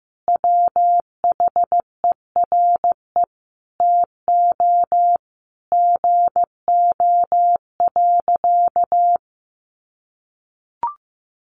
Morse code, heard as a keyed tone, about 15 wpm, 700 Hz. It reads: WHERE TO GO.
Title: Sentences from Top 700 Words 15wpm